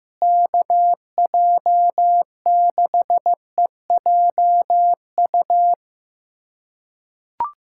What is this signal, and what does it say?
Morse code, heard as a keyed tone, about 15 wpm, 700 Hz. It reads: KJ6EJU
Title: USA Callsign DE Callsign 15wpm